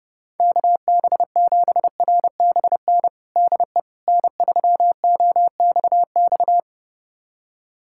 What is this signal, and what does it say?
Morse code, heard as a keyed tone, about 32 wpm, 700 Hz. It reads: KB7RBN DE N3OXX
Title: USA Callsign DE Callsign 30wpm